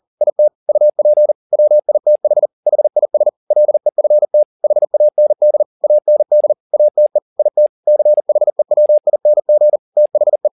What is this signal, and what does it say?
Morse code, heard as a keyed tone, about 40 wpm, 600 Hz. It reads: IT UP WITH HIS LEFT HAND AND ATE IT CHEWING THE
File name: the old man and the sea 2 40wpm